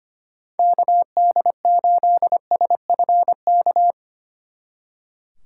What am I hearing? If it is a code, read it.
Morse code, heard as a keyed tone, about 30 wpm, 700 Hz. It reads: KD8SFK
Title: Speed Racing - USA 2x3 Call Signs 25wpm